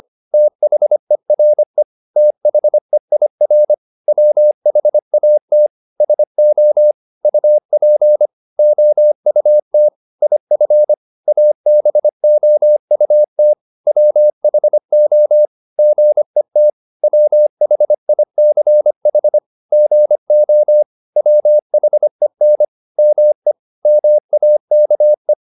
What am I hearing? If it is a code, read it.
Morse code, heard as a keyed tone, about 25 wpm, 600 Hz. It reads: THERE THEIR WHAT SO UP OUT IF ABOUT WHO GET WHICH GO WHEN ME MAKE
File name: Top-100-Words-25